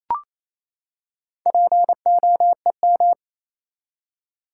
Morse code, heard as a keyed tone, about 28 wpm, 700 Hz. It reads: POEM
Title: Four Letter Words - 28wpm